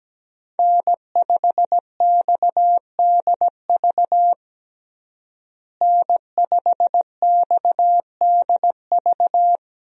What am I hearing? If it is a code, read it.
Morse code, heard as a keyed tone, about 17 wpm, 700 Hz. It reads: N5XDV N5XDV
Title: USA Call Signs - 5 Characters - Repeated 2x - 17wpm